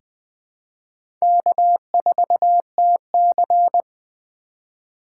Morse code, heard as a keyed tone, about 20 wpm, 700 Hz. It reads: K4TC
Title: USA Call Signs - 4 Characters - 2x Character Spacing - 20wpm